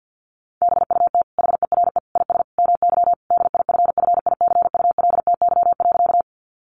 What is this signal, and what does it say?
Morse code, heard as a keyed tone, about 50 wpm, 700 Hz. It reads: BUT HERE IS MY DIFFICULTY.
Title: Sentences from Top 1500 Words 50wpm